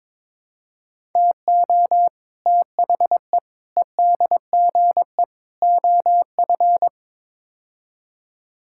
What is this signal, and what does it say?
Morse code, heard as a keyed tone, about 22 wpm, 700 Hz. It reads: TO THE EDGE OF
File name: practice / 22wpm / Build A Phrase - 5-Word Phrases - 22wpm